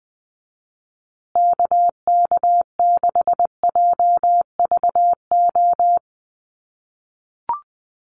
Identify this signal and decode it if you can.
Morse code, heard as a keyed tone, about 20 wpm, 700 Hz. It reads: KK6JVO
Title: USA Call Signs Repeated 3x 20wpm